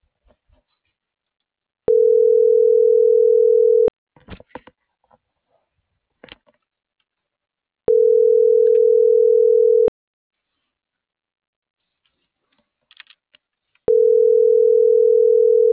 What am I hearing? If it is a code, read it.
Ringback tone (North America), 440 + 480 Hz.